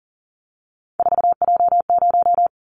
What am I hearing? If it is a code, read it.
Morse code, heard as a keyed tone, about 40 wpm, 700 Hz. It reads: 4J0
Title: Callsign Prefixes for Top 100 DXCC - 40wpm